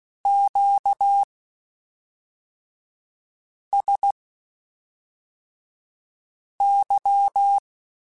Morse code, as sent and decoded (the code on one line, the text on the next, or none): --.- ... -.--
QSY